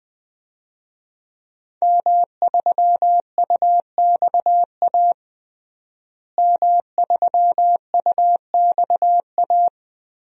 Morse code, heard as a keyed tone, about 21 wpm, 700 Hz. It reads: M3UXA M3UXA
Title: DXCC-100 Call Signs - 5 Characters - Repeated 2x - 20wpm